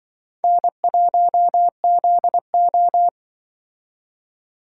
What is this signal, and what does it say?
Morse code, heard as a keyed tone, about 24 wpm, 700 Hz. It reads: N1ZO
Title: Speed Racing - USA 1x2 Call Signs 20wpm